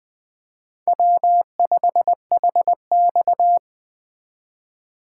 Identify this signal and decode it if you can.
Morse code, heard as a keyed tone, about 20 wpm, 700 Hz. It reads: W5HX
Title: USA Call Signs - 4 Characters - 2x Character Spacing - 20wpm